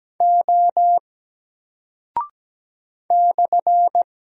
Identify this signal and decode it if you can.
Morse code, heard as a keyed tone, about 17 wpm, 700 Hz. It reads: O /
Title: Single Character (Letter:Number:Punctuation:Prosign) 17wpm